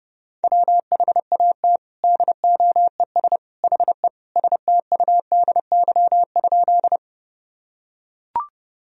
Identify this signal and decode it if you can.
Morse code, heard as a keyed tone, about 32 wpm, 700 Hz. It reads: WHAT DOES HE STUDY?
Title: Sentences from Top 1000 Words - Review Part 1 - 30wpm